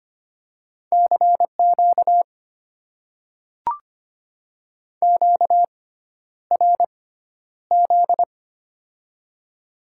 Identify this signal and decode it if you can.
Morse code, heard as a keyed tone, about 25 wpm, spaced as for 11 wpm, 700 Hz. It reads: CQ QRZ
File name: Top 90 QSO Elements - 6x Character Spacing - 25wpm